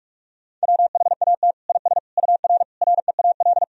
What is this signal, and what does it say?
Morse code, heard as a keyed tone, about 46 wpm, 700 Hz. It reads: WHAT IS UR REAL
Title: Sentences from Top 300 Words and Abbreviations 45wpm